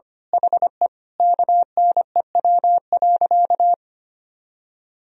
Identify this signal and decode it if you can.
Morse code, heard as a keyed tone, about 25 wpm, 700 Hz. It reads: HE KNEW.
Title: Sentences from Top 400 Words and Abbreviations 25wpm